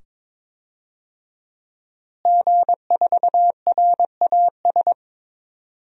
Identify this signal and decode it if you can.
Morse code, heard as a keyed tone, about 22 wpm, 700 Hz. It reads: G4RAS
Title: DXCC-100 Call Signs - 5 Characters - Repeated 2x - 22wpm